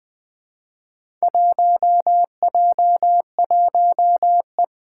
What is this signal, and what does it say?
Morse code, heard as a keyed tone, about 20 wpm, 700 Hz. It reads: 1J1E
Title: Four Challenging Letter-Number Combinations 20wpm